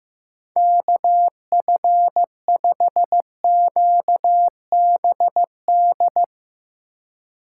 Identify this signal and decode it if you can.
Morse code, heard as a keyed tone, about 15 wpm, 700 Hz. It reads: KF5QBD
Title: USA Call Signs - 6 Characters - 4x Character Spacing - 15wpm